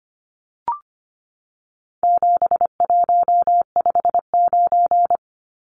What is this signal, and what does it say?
Morse code, heard as a keyed tone, about 25 wpm, 700 Hz. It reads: 7159
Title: Numbers - 4-Digit 25wpm